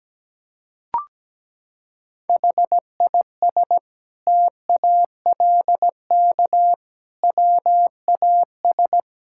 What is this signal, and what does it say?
Morse code, heard as a keyed tone, about 17 wpm, 700 Hz. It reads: HIS TALK WAS
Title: Sentences from Top 1000 Words - Review Part 6 - 17wpm